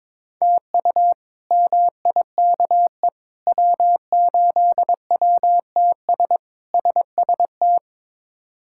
Morse code, heard as a keyed tone, about 22 wpm, 700 Hz. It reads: TU MIKE W8WTS SST
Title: K1USN SST QSO Practice - 22wpm